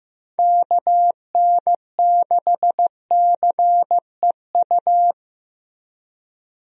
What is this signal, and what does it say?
Morse code, heard as a keyed tone, about 15 wpm, 700 Hz. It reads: KN6CEU
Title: USA Call Signs - 6 Characters - Repeated 2x - 15wpm